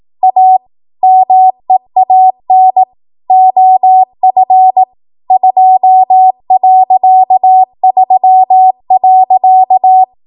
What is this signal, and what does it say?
Morse code, heard as a keyed tone, about 18 wpm, 750 Hz. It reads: A MEAN OF 2.3.
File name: Morse / 18WPM